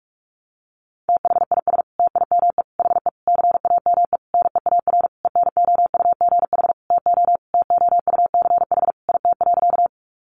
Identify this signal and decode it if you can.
Morse code, heard as a keyed tone, about 44 wpm, 700 Hz. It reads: THIS TIME HE CAME NEAR ENOUGH TO TOUCH IT.
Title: Sentences from Top 1200 Words 45wpm